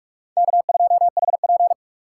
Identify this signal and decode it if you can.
Morse code, heard as a keyed tone, about 44 wpm, 700 Hz. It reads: K2HP